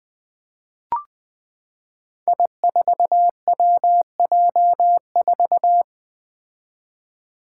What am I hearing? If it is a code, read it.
Morse code, heard as a keyed tone, about 20 wpm, 700 Hz. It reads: I4WJ4
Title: Five Challenging Letter-Number Combinations 20wpm